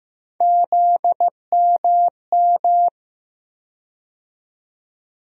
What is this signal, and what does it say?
Morse code, heard as a keyed tone, about 15 wpm, 700 Hz. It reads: ZMM
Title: USA Call Sign Postfixes - 3 Letters 15wpm